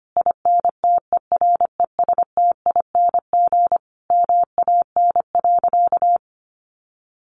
Morse code, heard as a keyed tone, about 25 wpm, 700 Hz. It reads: INTERESTING MAN.
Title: Sentences from Top 1000 Words 25wpm